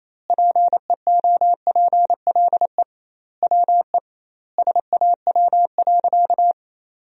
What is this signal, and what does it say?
Morse code, heard as a keyed tone, about 27 wpm, 700 Hz. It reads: PEOPLE WE SAW.
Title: 2x Word-Spacing - Sentences from Top 300 Words - 28wpm